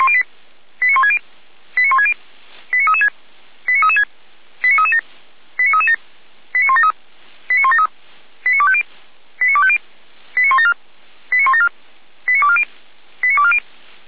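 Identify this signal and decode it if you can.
Five-tone selective call (ZVEI), heard as quick runs of five tones, tones of about 70 ms. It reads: 89160 89160 89336 89336 89337 89337 89173 89173 89250 89250 89174 89174 89240 89240
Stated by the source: ZVEI five-tone selective call received on an SDR.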